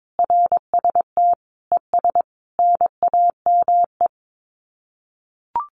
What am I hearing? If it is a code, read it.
Morse code, heard as a keyed tone, about 22 wpm, 700 Hz. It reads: RST ES NAME
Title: Common QSO Phrases 22wpm